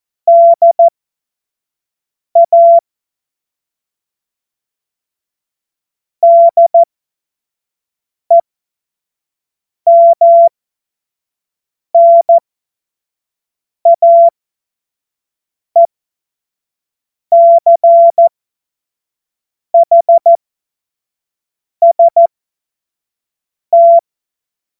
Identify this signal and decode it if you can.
Morse code, heard as a keyed tone, about 14 wpm, spaced as for 5 wpm, 680 Hz. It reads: DA DEMNAECHST
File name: Morsekurs / Witze / Witz 22. August 2023 25 F70